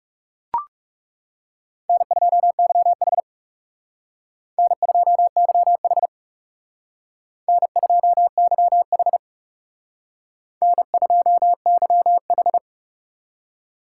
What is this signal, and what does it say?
Morse code, heard as a keyed tone, about 38 wpm, 700 Hz. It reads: N2YH N2YH N2YH N2YH